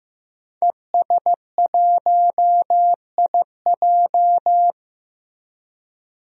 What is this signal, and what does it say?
Morse code, heard as a keyed tone, about 15 wpm, 700 Hz. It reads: ES1IJ
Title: DXCC-100 Call Signs - 5 Characters - 15wpm